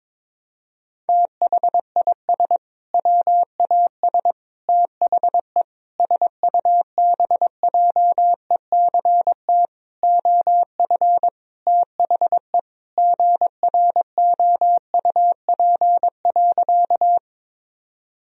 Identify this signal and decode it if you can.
Morse code, heard as a keyed tone, about 22 wpm, 700 Hz. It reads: THIS WAS THE SUBJECT OF THE GROUP.
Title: Sentences from Top 1000 Words - Review Part 1 - 22wpm